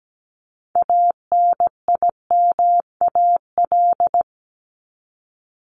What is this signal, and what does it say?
Morse code, heard as a keyed tone, about 17 wpm, 700 Hz. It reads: ANIMAL
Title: ES:S Suffix - Top 100 Words with Base Words 17wpm